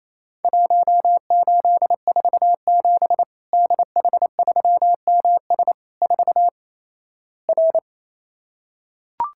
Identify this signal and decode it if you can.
Morse code, heard as a keyed tone, about 28 wpm, 700 Hz. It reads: 1847 DH3MS 4 R
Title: Worked All Europe DX Contest - QTC Exchange - 28wpm